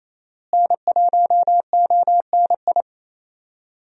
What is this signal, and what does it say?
Morse code, heard as a keyed tone, about 28 wpm, 700 Hz. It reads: N1ONI
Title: USA Call Signs - 5 Characters - Repeated 3x - 28wpm